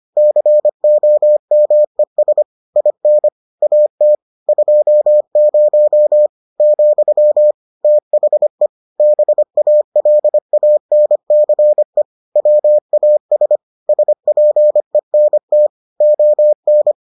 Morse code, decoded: COMES IN AT 20, THE BALANCE WAS SPENT ON

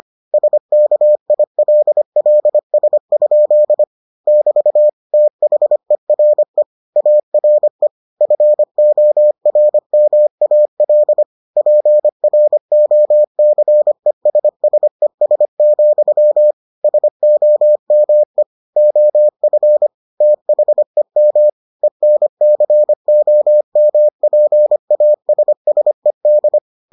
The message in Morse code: SKILLS? <BT> THERE ARE FORMAL PROCESSES, SOME OF THEM ENCOMPASSED

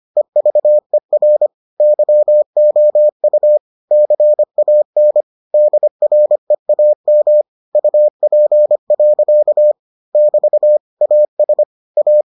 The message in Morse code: EVER YOU CAN DREAM UP. <BT> AS A

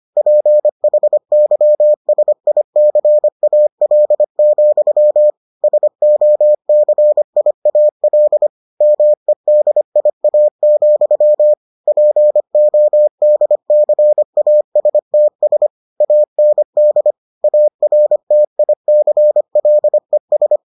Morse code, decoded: PHYSICAL, SOCIAL MEDIA, PODCASTS AND ARTICLES

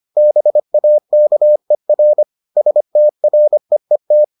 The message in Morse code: BAKER STREET

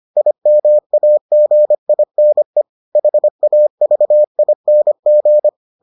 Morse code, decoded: IMAGINE HAVING

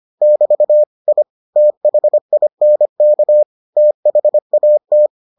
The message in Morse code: <BT> I THINK THAT

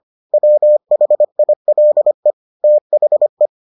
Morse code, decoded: WHILE THE

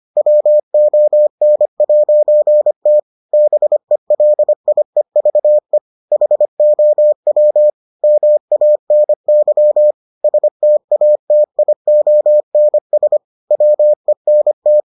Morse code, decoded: WON'T BELIEVE HOW MANY STATIONS WENT

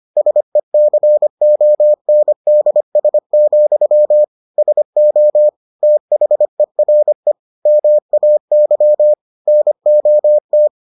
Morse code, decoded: SECONDS, SO THERE MAY NOT

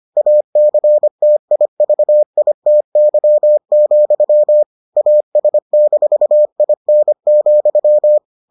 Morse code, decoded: ACTIVITY, AS-IN,

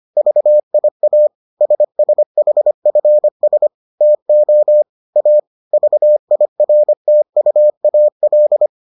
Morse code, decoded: VIA SSHFS TO A VIRTUAL

